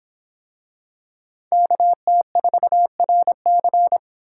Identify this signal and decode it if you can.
Morse code, heard as a keyed tone, about 24 wpm, 700 Hz. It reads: KT4RC